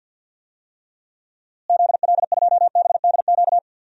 Morse code, heard as a keyed tone, about 47 wpm, 700 Hz. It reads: ZL2BDQ
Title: DXCC-100 Call Signs - 6 Characters - Repeated 2x - 50wpm